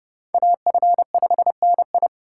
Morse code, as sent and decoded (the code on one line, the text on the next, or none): .- ..-. ..... -. ..
AF5NI